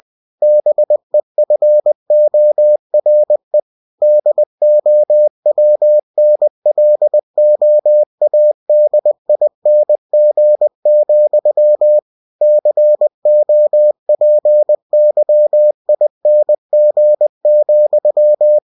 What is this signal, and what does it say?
Morse code, heard as a keyed tone, about 20 wpm, 600 Hz. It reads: BEFORE DOWNLOADING, COPYING,
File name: cw-books / winnie-the-pooh / winnie-20wpm-page-0114